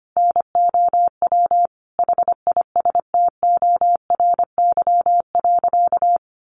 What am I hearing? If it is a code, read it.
Morse code, heard as a keyed tone, about 25 wpm, 700 Hz. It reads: NOW HISTORY.
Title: Sentences from Top 1500 Words 25wpm